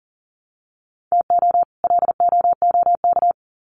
Morse code, heard as a keyed tone, about 40 wpm, 700 Hz. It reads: TO LOOK
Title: Sets of 2 Words 40wpm